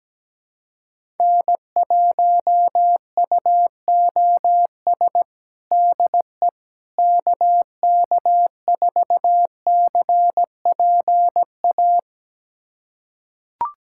Morse code, heard as a keyed tone, about 17 wpm, 700 Hz. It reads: N1UOS DE KK4CPA
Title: USA Callsign DE Callsign 17wpm